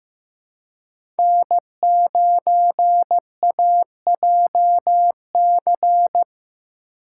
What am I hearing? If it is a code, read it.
Morse code, heard as a keyed tone, about 15 wpm, 700 Hz. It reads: N9AJC
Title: USA Call Signs - 5 Characters - 2x Character Spacing - 15wpm